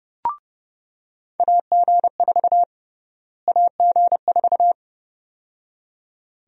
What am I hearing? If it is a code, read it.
Morse code, heard as a keyed tone, about 31 wpm, 700 Hz. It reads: AG4 AG4
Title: USA Call Sign Prefixes - Repeated 2x - 30wpm